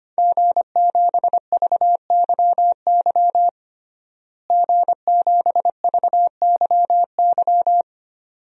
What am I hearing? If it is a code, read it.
Morse code, heard as a keyed tone, about 25 wpm, 700 Hz. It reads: G7VYY G7VYY
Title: DXCC-100 Call Signs - 5 Characters - Repeated 2x - 25wpm